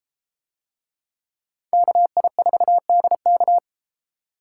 Morse code, heard as a keyed tone, about 32 wpm, 700 Hz. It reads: KI4DK